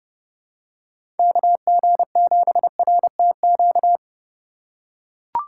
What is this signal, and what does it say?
Morse code, heard as a keyed tone, about 30 wpm, 700 Hz. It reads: KG7RTQ
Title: USA Call Signs - 6 Characters - 3x Character Spacing - 30wpm